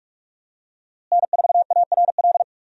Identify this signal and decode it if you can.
Morse code, heard as a keyed tone, about 41 wpm, 700 Hz. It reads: N4ARL
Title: USA Call Signs - 5 Characters - Repeated 2x - 45wpm